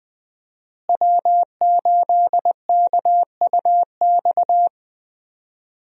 Morse code, heard as a keyed tone, about 20 wpm, 700 Hz. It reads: W8KUX
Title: USA Call Signs - 5 Characters - Repeated 2x - 20wpm